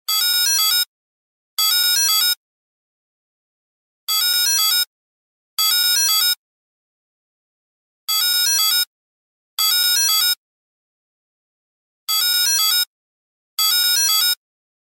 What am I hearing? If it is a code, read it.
There is no code.